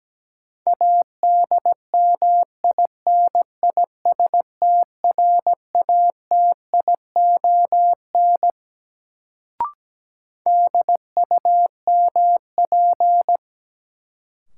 Morse code, heard as a keyed tone, about 17 wpm, 700 Hz. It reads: ADMINISTRATION DUMP
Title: Sentences from Top 1300 Words - New Word Drill - 17wpm